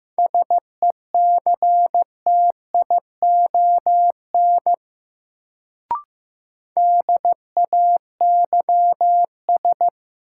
Morse code, heard as a keyed tone, about 15 wpm, 700 Hz. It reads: SECTION DAYS
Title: Sentences from Top 1000 Words - Review Part 5 - New Word Drill - 15wpm